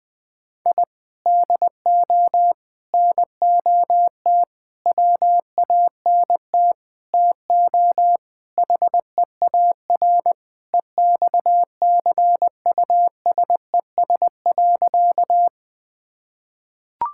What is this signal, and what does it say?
Morse code, heard as a keyed tone, about 20 wpm, 700 Hz. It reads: I DO NOT WANT TO HEAR EXCUSES.
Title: Sentences with 5-Word Phrases - 20wpm